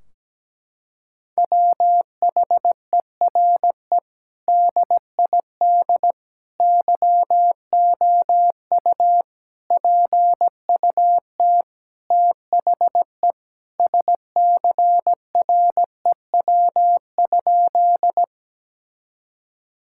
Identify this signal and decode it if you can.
Morse code, heard as a keyed tone, about 17 wpm, 700 Hz. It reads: WHERE DID YOU PUT THE SCREW?
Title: Sentences from Top 1600 Words 17wpm